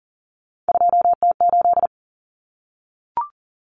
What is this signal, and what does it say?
Morse code, heard as a keyed tone, about 40 wpm, 700 Hz. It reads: 2T8
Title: Common Cut Numbers - 3-Digit 40wpm